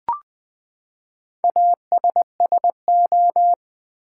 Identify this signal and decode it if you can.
Morse code, heard as a keyed tone, about 20 wpm, 700 Hz. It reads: ASSO
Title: Build A Word - 4 Letters At A Time - 11-Letter Words - 20wpm